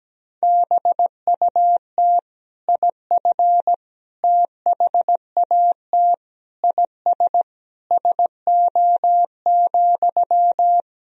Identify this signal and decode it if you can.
Morse code, heard as a keyed tone, about 17 wpm, 700 Hz. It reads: BUT IF THAT IS SO,
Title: Sentences from Top 200 Words and Abbreviations 17wpm